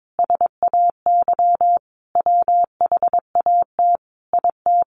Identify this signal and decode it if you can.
Morse code, heard as a keyed tone, about 22 wpm, 700 Hz. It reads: SAY WHAT IT
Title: Sentences from Top 600 Words 22wpm